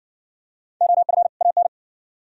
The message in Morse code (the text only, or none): OF AN